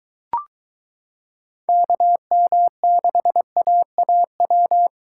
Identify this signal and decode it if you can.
Morse code, heard as a keyed tone, about 23 wpm, 700 Hz. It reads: KM6AAW